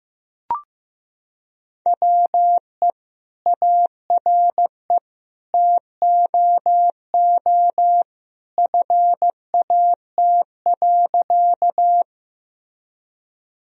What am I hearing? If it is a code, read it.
Morse code, heard as a keyed tone, about 15 wpm, 700 Hz. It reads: WE ARE TOO FAT.
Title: Sentences from Top 1000 Words - Review Part 4 - 15wpm